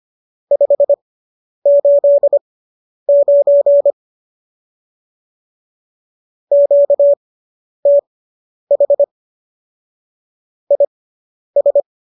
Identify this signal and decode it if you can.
Morse code, heard as a keyed tone, about 25 wpm, spaced as for 10 wpm, 570 Hz. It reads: 589 QTH IS